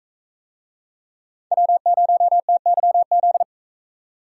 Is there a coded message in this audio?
Yes: Morse code from 1.5 to 3.4 s, heard as a keyed tone, about 38 wpm, 700 Hz. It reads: W0TYZ